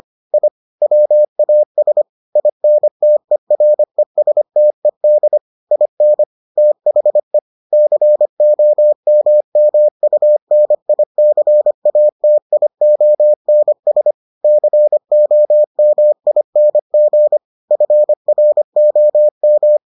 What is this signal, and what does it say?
Morse code, heard as a keyed tone, about 25 wpm, 600 Hz. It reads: I WAS INTERESTED IN THE COMMUNICATIONS COMING FROM